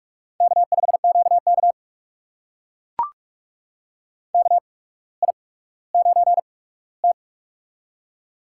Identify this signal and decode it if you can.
Morse code, heard as a keyed tone, about 44 wpm, spaced as for 19 wpm, 700 Hz. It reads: K5QK KI9T
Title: USA Call Signs - 4 Characters - 8x Character Spacing - 45wpm